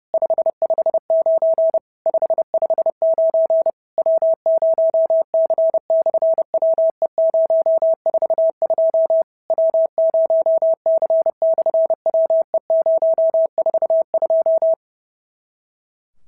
Morse code, heard as a keyed tone, about 30 wpm, 650 Hz. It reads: 559 559 W0C/WE042 W0C/WE042
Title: SOTA QSO Practice 30wpm